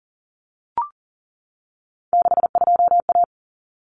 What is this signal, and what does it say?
Morse code, heard as a keyed tone, about 40 wpm, 700 Hz. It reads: 62A
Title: Common Cut Numbers - 3-Digit 40wpm